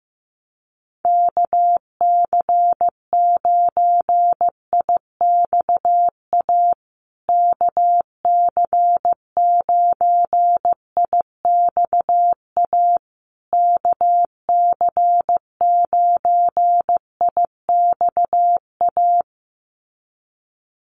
Morse code, heard as a keyed tone, about 15 wpm, 700 Hz. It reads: KC9IXA KC9IXA KC9IXA
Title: USA Call Signs Repeated 3x 15wpm